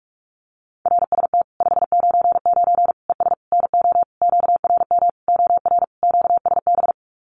Morse code, heard as a keyed tone, about 44 wpm, 700 Hz. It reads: RST 599 ES NO QRM OR QSB
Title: Common QSO Phrases 45wpm